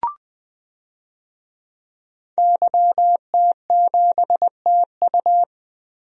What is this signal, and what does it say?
Morse code, heard as a keyed tone, about 20 wpm, 700 Hz. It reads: YT7TU
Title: DXCC-100 Call Signs - 5 Characters - 20wpm